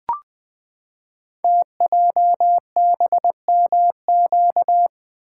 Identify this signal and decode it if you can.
Morse code, heard as a keyed tone, about 20 wpm, 700 Hz. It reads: TJBMQ
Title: Code Groups - 5 Letters - 20wpm